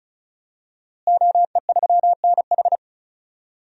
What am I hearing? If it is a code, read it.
Morse code, heard as a keyed tone, about 34 wpm, 700 Hz. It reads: OE3NH
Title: DXCC-100 Call Signs - 5 Characters - Repeated 2x - 35wpm